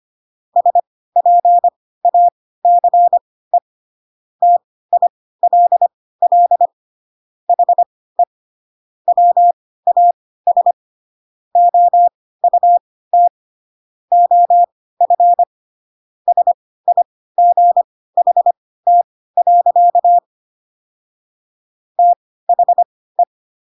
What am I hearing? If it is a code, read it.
Morse code, heard as a keyed tone, about 25 wpm, spaced as for 16 wpm, 700 Hz. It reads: SPACE TILL HE WAS OUT OF SIGHT. THE